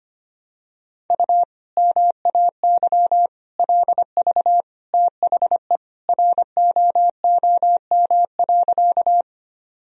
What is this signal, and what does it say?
Morse code, heard as a keyed tone, about 25 wpm, 700 Hz. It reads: U MAY LV THE ROOM.
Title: Sentences from Top 400 Words and Abbreviations 25wpm